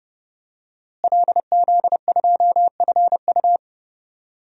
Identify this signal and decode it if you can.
Morse code, heard as a keyed tone, about 32 wpm, 700 Hz. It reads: LZ2FU
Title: DXCC-100 Call Signs - 5 Characters - Repeated 2x - 30wpm